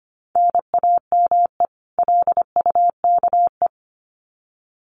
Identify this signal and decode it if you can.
Morse code, heard as a keyed tone, about 25 wpm, 700 Hz. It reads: NAME LUKE
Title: Common QSO Phrases 25wpm